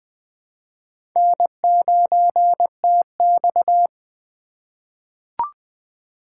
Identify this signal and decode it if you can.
Morse code, heard as a keyed tone, about 20 wpm, 700 Hz. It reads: N9TX
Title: USA Call Signs - 4 Characters - 3x Character Spacing - 20wpm